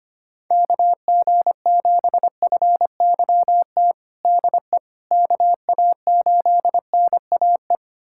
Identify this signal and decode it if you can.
Morse code, heard as a keyed tone, about 25 wpm, 700 Hz. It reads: KG7FYT DE KA8NAE
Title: USA Callsign DE Callsign - 6 Characters - 25wpm